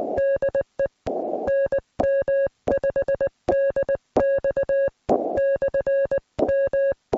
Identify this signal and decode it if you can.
Morse code, heard as a keyed tone, about 19 wpm, spaced as for 16 wpm, 560 Hz. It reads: DE NM5DX/M